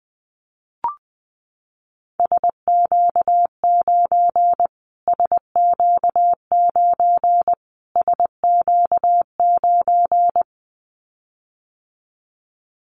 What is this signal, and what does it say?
Morse code, heard as a keyed tone, about 20 wpm, 700 Hz. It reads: SQ9 SQ9 SQ9
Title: Callsign Prefixes for Top 25 DXCC - Repeated 3x - 20wpm